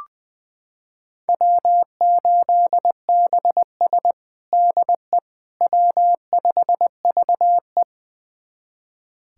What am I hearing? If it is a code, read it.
Morse code, heard as a keyed tone, about 21 wpm, 700 Hz. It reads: W8BS DE W5VE
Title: USA Callsign DE Callsign - 4 Characters - 20wpm